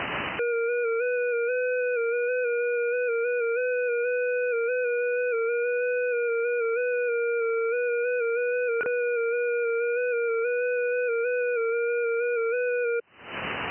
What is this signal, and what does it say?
FT8.
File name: IC-7100